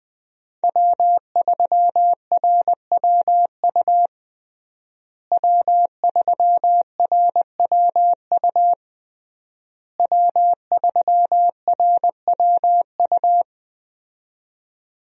Morse code, heard as a keyed tone, about 21 wpm, 700 Hz. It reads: W3RWU W3RWU W3RWU
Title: USA Call Signs - 5 Characters - Repeated 3x - 20wpm